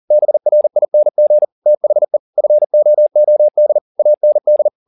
Morse code, heard as a keyed tone, about 40 wpm, 600 Hz. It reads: BRING THE FOOD AND
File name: the old man and the sea 2 40wpm